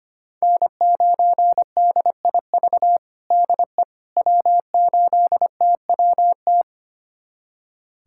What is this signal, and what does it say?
Morse code, heard as a keyed tone, about 25 wpm, 700 Hz. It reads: N9DIV DE W8TWT
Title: USA Callsign DE Callsign - 5 Characters - 25wpm